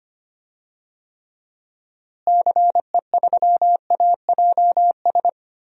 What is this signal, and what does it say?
Morse code, heard as a keyed tone, about 25 wpm, 700 Hz. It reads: CE3AJS
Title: DXCC-100 Call Signs - 6 Characters - Repeated 3x - 25wpm